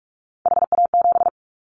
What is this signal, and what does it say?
Morse code, heard as a keyed tone, about 46 wpm, 700 Hz. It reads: HA7
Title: Callsign Prefixes for Top 25 DXCC - 45wpm